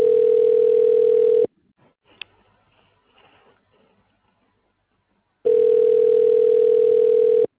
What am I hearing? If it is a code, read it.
Ringback tone (North America), 440 + 480 Hz.